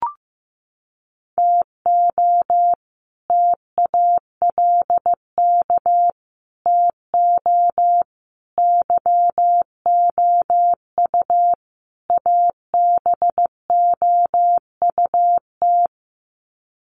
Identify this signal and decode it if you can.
Morse code, heard as a keyed tone, about 15 wpm, 700 Hz. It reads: TO TALK TO YOU ABOUT
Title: Sets of 5 Words Encore 15wpm